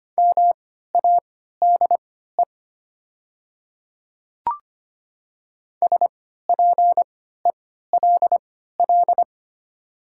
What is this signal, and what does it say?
Morse code, heard as a keyed tone, about 25 wpm, spaced as for 14 wpm, 700 Hz. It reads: MADE SPELL
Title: 3x Character Spacing - Top 300 Words - 25wpm